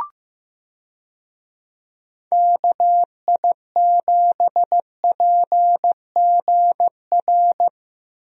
Morse code, heard as a keyed tone, about 16 wpm, 700 Hz. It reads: KI7PGR
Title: USA Call Signs - 6 Characters - Repeated 3x - 15wpm